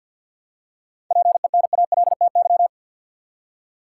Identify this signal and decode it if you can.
Morse code, heard as a keyed tone, about 50 wpm, 700 Hz. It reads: PENALTY
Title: Seven Letter Words - 50wpm